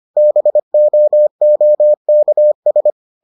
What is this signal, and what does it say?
Morse code, heard as a keyed tone, about 25 wpm, 600 Hz. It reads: BOOKS